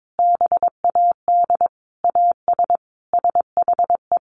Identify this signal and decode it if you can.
Morse code, heard as a keyed tone, about 22 wpm, 700 Hz. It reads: BAD AS SHE
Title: Sentences from Top 1300 Words 22wpm